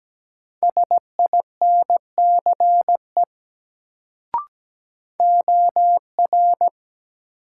Morse code, heard as a keyed tone, about 17 wpm, 700 Hz. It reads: SINCE OR
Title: Sets of 2 Words - New Word Drill - 17wpm